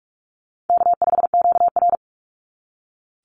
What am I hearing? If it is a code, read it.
Morse code, heard as a keyed tone, about 44 wpm, 700 Hz. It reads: K5QR
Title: NA SOTA Chaser Callsigns 45wpm